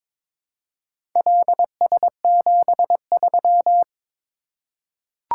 Morse code, heard as a keyed tone, about 22 wpm, 700 Hz. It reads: LS73
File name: practice / 22wpm / Four Letter-Number Combinations - 22wpm